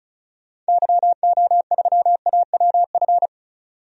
Morse code, heard as a keyed tone, about 34 wpm, 700 Hz. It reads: YO3AWF
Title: DXCC-100 Call Signs - 6 Characters - 35wpm